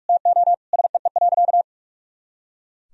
Morse code, heard as a keyed tone, about 42 wpm, 700 Hz. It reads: TO SEE.
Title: Sentences from Top 200 Words and Abbreviations 45wpm